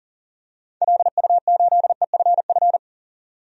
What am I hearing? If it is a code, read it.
Morse code, heard as a keyed tone, about 40 wpm, 700 Hz. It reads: LU8EFF
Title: DXCC-100 Call Signs - 6 Characters - 40wpm